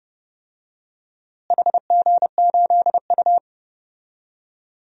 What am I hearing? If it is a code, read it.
Morse code, heard as a keyed tone, about 28 wpm, 700 Hz. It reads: HG8U
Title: DXCC-100 Call Signs - 4 Characters - Repeated 3x - 30wpm